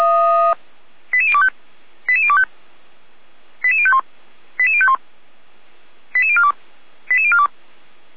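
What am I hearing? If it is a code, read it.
Five-tone selective call (ZVEI), heard as quick runs of five tones, tones of about 70 ms. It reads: 88025 88025 88051 88051 88042 88042
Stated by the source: ZVEI five-tone call 88129 received by SDR, logged by Feuerwehr Orlamünde.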